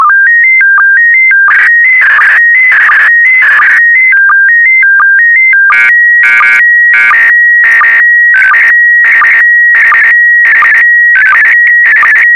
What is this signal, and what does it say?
SSTV (slow-scan television), PD 90.
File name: sstv / PD90-VK4MSL